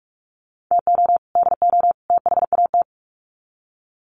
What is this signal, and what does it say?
Morse code, heard as a keyed tone, about 44 wpm, 700 Hz. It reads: TO DO THAT
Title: Sets of 3 Words 45wpm